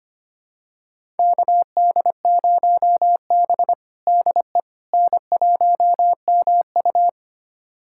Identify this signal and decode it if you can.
Morse code, heard as a keyed tone, about 25 wpm, 700 Hz. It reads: KD0B DE N1MU
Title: USA Callsign DE Callsign - 4 Characters - 25wpm